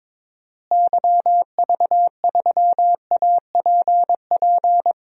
Morse code, heard as a keyed tone, about 22 wpm, 700 Hz. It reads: YV3APP
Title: DXCC-100 Call Signs - 6 Characters - Repeated 2x - 22wpm